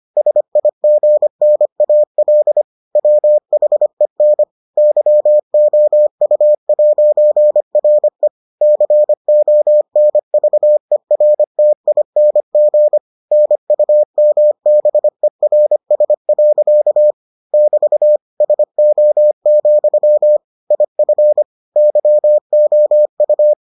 Morse code, heard as a keyed tone, about 25 wpm, 600 Hz. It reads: SIGNAL WHEN YOU'RE CONVERTING NUMBERS. <BT> SO, IF YOU